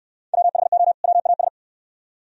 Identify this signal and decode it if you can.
Morse code, heard as a keyed tone, about 64 wpm, 700 Hz. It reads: WHO WAS